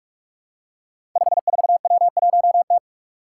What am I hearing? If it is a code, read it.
Morse code, heard as a keyed tone, about 44 wpm, 700 Hz. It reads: 54W1T
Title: Five Challenging Letter-Number Combinations 45wpm